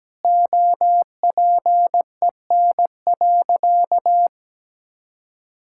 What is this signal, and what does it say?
Morse code, heard as a keyed tone, about 17 wpm, 700 Hz. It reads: OPEN.
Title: Sentences from Top 1000 Words - Review Part 6 - 17wpm